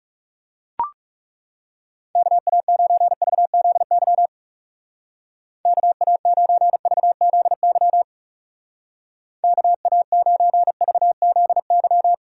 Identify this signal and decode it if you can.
Morse code, heard as a keyed tone, about 36 wpm, 700 Hz. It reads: KA9VZY KA9VZY KA9VZY
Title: Speed Racing - USA 2x3 Call Signs 30wpm